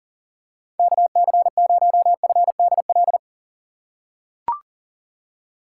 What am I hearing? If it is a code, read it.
Morse code, heard as a keyed tone, about 40 wpm, 700 Hz. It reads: KC0FDL
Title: USA Call Signs - 6 Characters - 6x Character Spacing - 40wpm